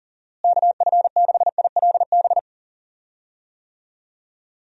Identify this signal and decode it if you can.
Morse code, heard as a keyed tone, about 40 wpm, 700 Hz. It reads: KF6ILB
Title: USA Call Signs - 6 Characters - Repeated 2x - 40wpm